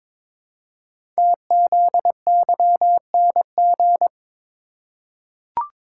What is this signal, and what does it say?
Morse code, heard as a keyed tone, about 22 wpm, 700 Hz. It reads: TZYNG
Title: Code Groups - 5 Letter:Numbers - 22wpm